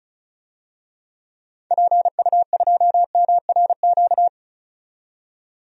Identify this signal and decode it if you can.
Morse code, heard as a keyed tone, about 34 wpm, 700 Hz. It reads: PU2MRQ
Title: DXCC-100 Call Signs - 6 Characters - 35wpm